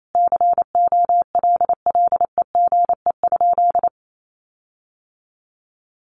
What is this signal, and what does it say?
Morse code, heard as a keyed tone, about 28 wpm, 700 Hz. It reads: COLLEGE?
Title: Sentences from Top 1300 Words 28wpm